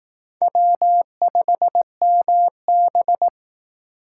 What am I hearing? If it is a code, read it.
Morse code, heard as a keyed tone, about 18 wpm, 700 Hz. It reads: W5MB
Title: Speed Racing - USA Call Signs 15wpm